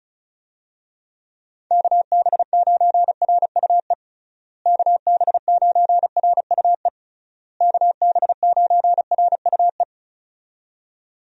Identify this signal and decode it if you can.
Morse code, heard as a keyed tone, about 36 wpm, 700 Hz. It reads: KB9RUE KB9RUE KB9RUE
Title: USA Call Signs - 6 Characters - Repeated 3x - 35wpm